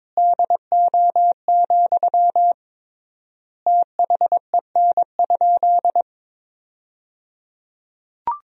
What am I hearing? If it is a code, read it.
Morse code, heard as a keyed tone, about 22 wpm, 700 Hz. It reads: DO, THEN?
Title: 3x Word-Spacing - Sentences from Top 100 Words - 22wpm